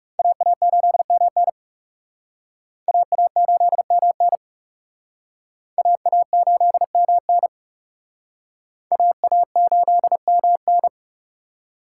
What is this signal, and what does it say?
Morse code, heard as a keyed tone, about 37 wpm, 700 Hz. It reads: AA8MN AA8MN AA8MN AA8MN
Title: Speed Racing - USA 2x2 Call Signs 30wpm